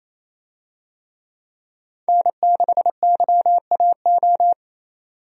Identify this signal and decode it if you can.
Morse code, heard as a keyed tone, about 27 wpm, 700 Hz. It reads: N6YAO
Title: USA Call Signs - 5 Characters - 28wpm